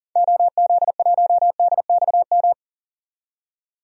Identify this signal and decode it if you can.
Morse code, heard as a keyed tone, about 39 wpm, 700 Hz. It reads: OZ1DXM
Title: DXCC-100 Call Signs - 6 Characters - 40wpm